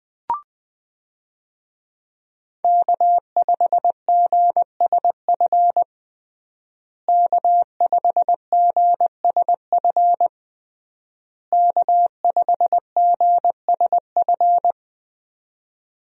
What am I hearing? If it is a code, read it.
Morse code, heard as a keyed tone, about 21 wpm, 700 Hz. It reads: K5GSF K5GSF K5GSF
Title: USA Call Signs - 5 Characters - Repeated 3x - 20wpm